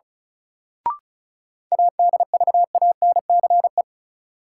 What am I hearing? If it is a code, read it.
Morse code, heard as a keyed tone, about 34 wpm, 700 Hz. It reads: ADVANCE
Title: Sentences from Top 1400 Words - New Word Drill - 35wpm